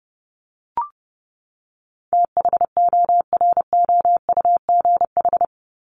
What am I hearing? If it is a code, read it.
Morse code, heard as a keyed tone, about 30 wpm, 700 Hz. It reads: THOROUGH
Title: LY Suffix - Top 100 Words with Base Words 30wpm